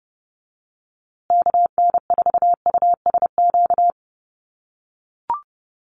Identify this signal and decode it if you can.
Morse code, heard as a keyed tone, about 30 wpm, 700 Hz. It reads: KN4USQ
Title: USA Call Signs Repeated 3x 30wpm